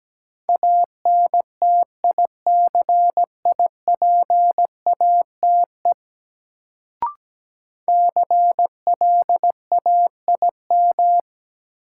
Morse code, heard as a keyed tone, about 17 wpm, 700 Hz. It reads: ANTICIPATE CLAIM
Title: Sentences from Top 1500 Words - New Word Drill Alt - 17wpm